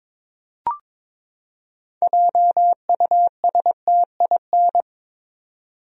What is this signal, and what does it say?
Morse code, heard as a keyed tone, about 22 wpm, 700 Hz. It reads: JUSTIN
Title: USA Common Names - 22wpm